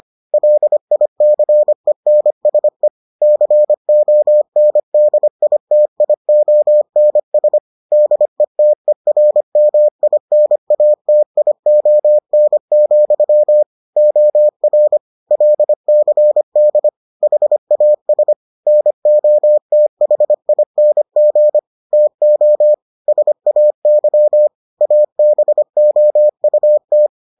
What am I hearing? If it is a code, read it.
Morse code, heard as a keyed tone, about 25 wpm, 600 Hz. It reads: LICENSE CONDITIONS DETERMINATION, OR LCD HAS NOTHING TO SAY ABOUT